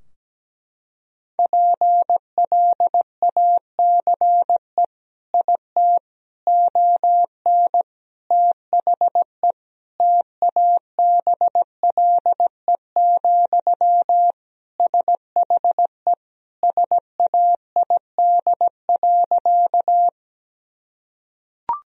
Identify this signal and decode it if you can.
Morse code, heard as a keyed tone, about 17 wpm, 700 Hz. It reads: PLACE IT ON THE TABLE, SHE SAID.
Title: Sentences from Top 1000 Words - Review Part 2 - 17wpm